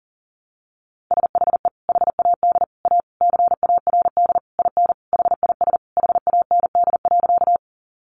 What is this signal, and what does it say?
Morse code, heard as a keyed tone, about 37 wpm, 700 Hz. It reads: SHE HAD A CARD IN HIS HAND.
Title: Sentences from Top 600 Words 40wpm